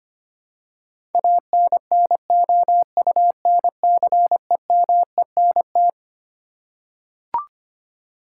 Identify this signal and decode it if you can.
Morse code, heard as a keyed tone, about 25 wpm, 700 Hz. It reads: ANNOUNCEMENT
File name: Twelve Letter Words - 25wpm